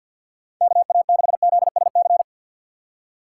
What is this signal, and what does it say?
Morse code, heard as a keyed tone, about 51 wpm, 700 Hz. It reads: KA6ZSC
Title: USA Call Signs - 4x Character Spacing - 50wpm